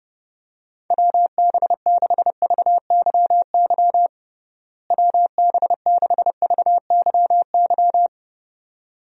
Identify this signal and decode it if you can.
Morse code, heard as a keyed tone, about 30 wpm, 700 Hz. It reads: WB6VYY WB6VYY
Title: USA Call Signs - 6 Characters - Repeated 2x - 30wpm